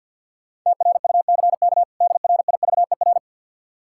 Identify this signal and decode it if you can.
Morse code, heard as a keyed tone, about 52 wpm, 700 Hz. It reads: TRUCK DRIVER
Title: Practice QSO - Exchange 3 - More Info - 50wpm